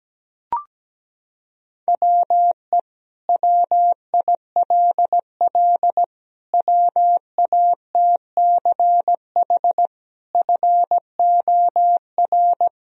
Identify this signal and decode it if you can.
Morse code, heard as a keyed tone, about 17 wpm, 700 Hz. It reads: WE WILL WATCH FOR
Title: Sentences from Top 1000 Words - Review Part 1 - 17wpm